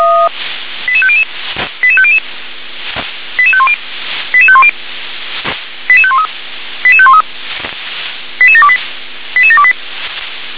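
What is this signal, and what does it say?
Five-tone selective call (ZVEI), heard as quick runs of five tones, tones of about 70 ms. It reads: 88500 88500 88510 88510 88513 88513 88728 88728
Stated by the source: ZVEI five-tone selective call 88716, received by SDR.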